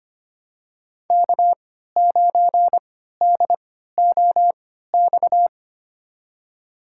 Morse code, heard as a keyed tone, about 25 wpm, spaced as for 14 wpm, 700 Hz. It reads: K9DOX
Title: USA Call Signs - 5 Characters - 3x Character Spacing - 25wpm